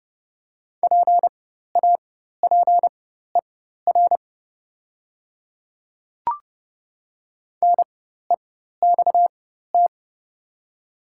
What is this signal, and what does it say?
Morse code, heard as a keyed tone, about 30 wpm, spaced as for 14 wpm, 700 Hz. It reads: PAPER NEXT
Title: 4x Character Spacing - Top 300 Words - 30wpm